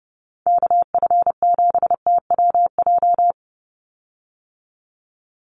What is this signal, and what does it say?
Morse code, heard as a keyed tone, about 30 wpm, 700 Hz. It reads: KF7TWJ
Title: USA Call Signs 30wpm